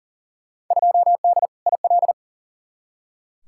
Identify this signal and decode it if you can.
Morse code, heard as a keyed tone, about 40 wpm, 700 Hz. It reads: 2D IL
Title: Field Day Exchanges - 40wpm